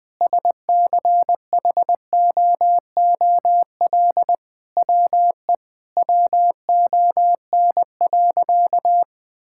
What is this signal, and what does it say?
Morse code, heard as a keyed tone, about 20 wpm, 700 Hz. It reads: SCHOOL WE WON.
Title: Sentences with 9-Word Phrases - Spoken Beginning - 20wpm